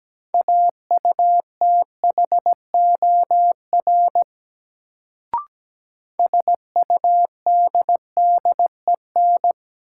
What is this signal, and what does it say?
Morse code, heard as a keyed tone, about 17 wpm, 700 Hz. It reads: AUTHOR SUDDEN
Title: Sets of 4 Words - New Word Drill - 17wpm